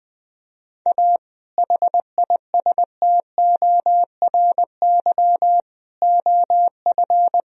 Morse code, heard as a keyed tone, about 20 wpm, 700 Hz. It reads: A HISTORY OF
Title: Sentences from Top 1000 Words - Review Part 3 - 20wpm